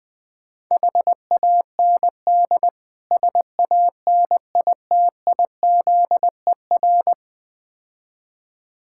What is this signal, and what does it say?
Morse code, heard as a keyed tone, about 20 wpm, 690 Hz. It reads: HAND SANITIZER
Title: Difficult Words with Hints - 20wpm